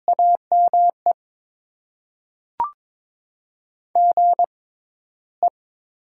Morse code, heard as a keyed tone, about 22 wpm, spaced as for 14 wpm, 700 Hz. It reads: AME GE
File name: Top 20 QSO Elements - 6x Character Spacing - 22wpm